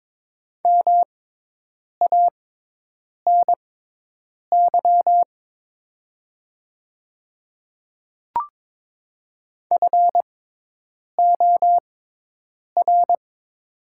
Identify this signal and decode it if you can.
Morse code, heard as a keyed tone, about 22 wpm, spaced as for 8 wpm, 700 Hz. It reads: MANY FOR
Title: 6x Character Spacing - Top 100 Words - 22wpm